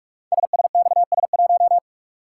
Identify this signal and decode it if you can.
Morse code, heard as a keyed tone, about 44 wpm, 700 Hz. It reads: SSXS1
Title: Code Groups - 5 Letter:Numbers - 45wpm